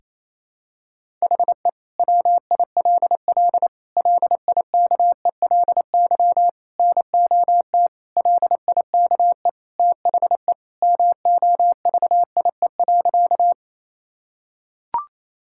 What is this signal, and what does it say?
Morse code, heard as a keyed tone, about 28 wpm, 700 Hz. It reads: HE WILL LIKELY NOT LIKE THE MOVIE.
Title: Sentences from Top 1000 Words - Review Part 5 - 28wpm